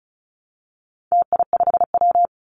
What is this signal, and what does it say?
Morse code, heard as a keyed tone, about 34 wpm, 700 Hz. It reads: TI5W
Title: Call Signs Repeated 2x - 35wpm